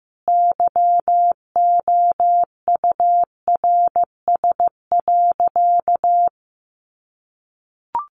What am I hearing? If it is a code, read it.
Morse code, heard as a keyed tone, about 15 wpm, 700 Hz. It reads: YOURS.
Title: A Sound Of Thunder 15wpm